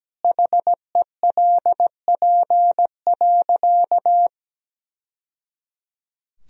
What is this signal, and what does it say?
Morse code, heard as a keyed tone, about 17 wpm, 700 Hz. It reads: HELP.
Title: Sentences from Top 1000 Words - Review Part 3 - 17wpm